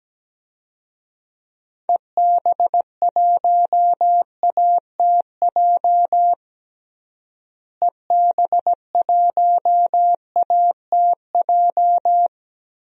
Morse code, heard as a keyed tone, about 17 wpm, 700 Hz. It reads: EB1ATJ EB1ATJ
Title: DXCC-100 Call Signs - 6 Characters - Repeated 2x - 17wpm